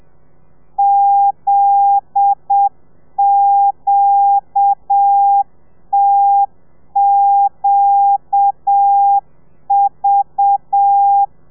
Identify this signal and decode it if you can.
Morse code, heard as a keyed tone, about 7 wpm, 800 Hz. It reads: ZQTQV